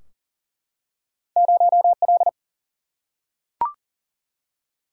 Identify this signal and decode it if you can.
Morse code, heard as a keyed tone, about 42 wpm, 700 Hz. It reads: 0L
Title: Single-Letter Single-Number 40wpm